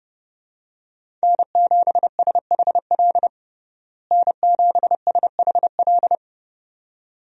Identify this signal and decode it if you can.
Morse code, heard as a keyed tone, about 30 wpm, 700 Hz. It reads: N7SHL N7SHL
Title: USA Call Signs - 5 Characters - Repeated 2x - 30wpm